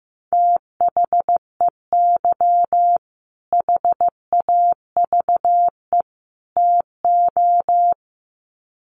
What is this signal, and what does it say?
Morse code, heard as a keyed tone, about 15 wpm, 700 Hz. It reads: THEY HAVE TO
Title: Sets of 3 Words 15wpm